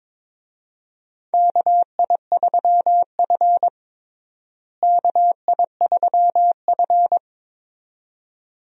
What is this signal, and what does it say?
Morse code, heard as a keyed tone, about 22 wpm, 700 Hz. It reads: KI3F KI3F
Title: USA Call Signs - 4 Characters - Repeated 2x - 22wpm